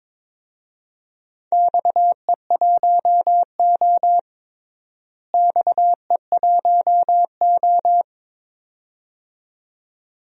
Morse code, heard as a keyed tone, about 22 wpm, 700 Hz. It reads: XE1O XE1O
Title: DXCC-100 Call Signs - 4 Characters - Repeated 2x - 22wpm